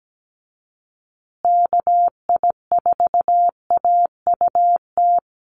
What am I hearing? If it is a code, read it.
Morse code, heard as a keyed tone, about 17 wpm, 700 Hz. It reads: KI4AUT
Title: USA Call Signs Repeated 2x 17wpm